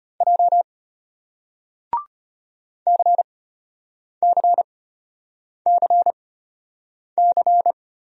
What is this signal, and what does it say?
Morse code, heard as a keyed tone, about 34 wpm, spaced as for 8 wpm, 700 Hz. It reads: J CCCC